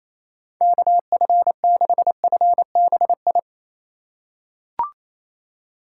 Morse code, heard as a keyed tone, about 28 wpm, 700 Hz. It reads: KF6FBI
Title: USA Call Signs - 6 Characters - 4x Character Spacing - 28wpm